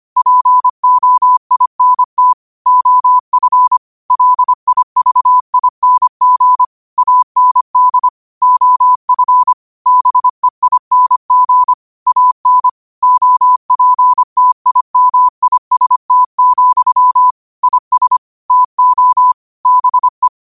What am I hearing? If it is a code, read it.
Morse code, heard as a keyed tone, about 25 wpm, 1000 Hz. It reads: POINT OF LIVING AND OF BEING AN OPTIMIST, IS TO BE
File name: qotdfiles / 25wpm